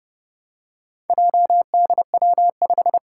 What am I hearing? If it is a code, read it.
Morse code, heard as a keyed tone, about 31 wpm, 700 Hz. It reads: JDW5
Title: Four Challenging Letter-Number Combinations 30wpm